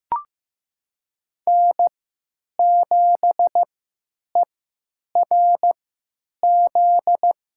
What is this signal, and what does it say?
Morse code, heard as a keyed tone, about 15 wpm, spaced as for 9 wpm, 700 Hz. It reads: N7ERZ
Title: USA Call Signs - 5 Characters - 3x Character Spacing - 15wpm